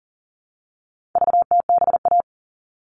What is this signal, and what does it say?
Morse code, heard as a keyed tone, about 39 wpm, 700 Hz. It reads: VTBA
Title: All Cut Numbers - 4-Digit 40wpm